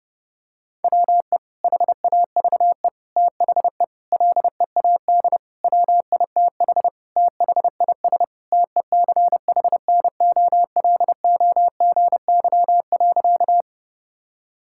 Morse code, heard as a keyed tone, about 30 wpm, 700 Hz. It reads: WE HAVE THE LEAD WITH THIS TECHNOLOGY.
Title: Sentences from Top 1000 Words - Review Part 5 - 30wpm